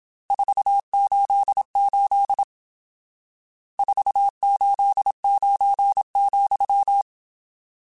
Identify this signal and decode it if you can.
Morse code, heard as a keyed tone, about 27 wpm, 800 Hz. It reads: 488 489,
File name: RSTs / RST-mp3s / Various RST @25